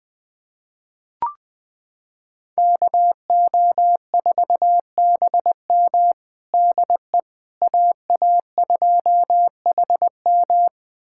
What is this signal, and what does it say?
Morse code, heard as a keyed tone, about 21 wpm, 700 Hz. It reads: KO4BM DE AA2HM
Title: USA Callsign DE Callsign - 5 Characters - 20wpm